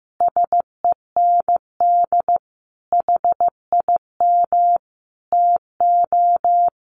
Morse code, heard as a keyed tone, about 15 wpm, 700 Hz. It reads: SEND HIM TO
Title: Sentences from Top 1500 Words 15wpm